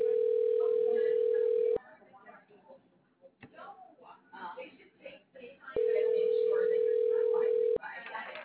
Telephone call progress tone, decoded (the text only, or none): ringback tone